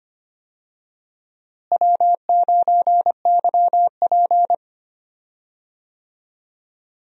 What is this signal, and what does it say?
Morse code, heard as a keyed tone, about 25 wpm, 700 Hz. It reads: W9YP
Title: USA Call Signs - 4 Characters - Repeated 3x - 25wpm